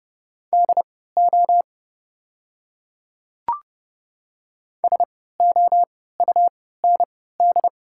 Morse code, heard as a keyed tone, about 30 wpm, spaced as for 17 wpm, 700 Hz. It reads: DO SOUND
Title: 3x Character Spacing - Top 100 Words - 30wpm